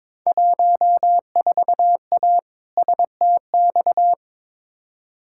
Morse code, heard as a keyed tone, about 22 wpm, 700 Hz. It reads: 14A STX
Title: Field Day Exchanges Encore - 22wpm